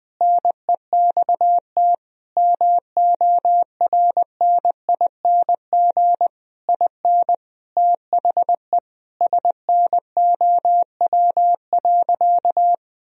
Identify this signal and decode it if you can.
Morse code, heard as a keyed tone, about 20 wpm, 700 Hz. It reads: NEXT MORNING IN THE SNOW.
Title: Sentences from Top 1000 Words - Review Part 3 - 20wpm